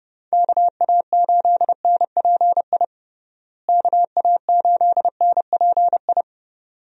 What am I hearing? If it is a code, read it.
Morse code, heard as a keyed tone, about 32 wpm, 700 Hz. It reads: KA8NPI KA8NPI
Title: USA Call Signs - 6 Characters - Repeated 2x - 30wpm